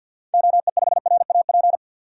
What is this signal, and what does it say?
Morse code, heard as a keyed tone, about 50 wpm, 700 Hz. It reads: OE5RAP
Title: DXCC-100 Call Signs - 6 Characters - 50wpm